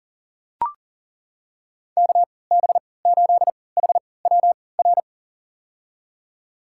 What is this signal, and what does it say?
Morse code, heard as a keyed tone, about 37 wpm, spaced as for 22 wpm, 700 Hz. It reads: KB8HWR
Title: USA Call Signs - 6 Characters - 3x Character Spacing - 40wpm